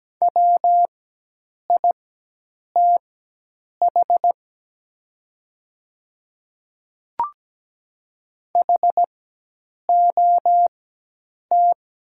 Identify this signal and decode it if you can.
Morse code, heard as a keyed tone, about 17 wpm, spaced as for 8 wpm, 700 Hz. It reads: WITH HOT
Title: 4x Character Spacing - Top 100 Words - 17wpm